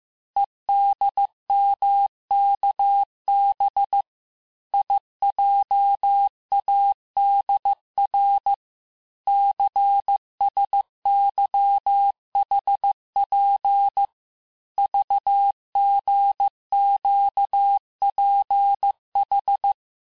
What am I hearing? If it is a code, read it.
Morse code, heard as a keyed tone, about 15 wpm, 800 Hz. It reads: EDMKB IJADR CSYHP VGQPH
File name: cours-du-lundi files / cours-1585-texte-chiffre-vitesse-14mpm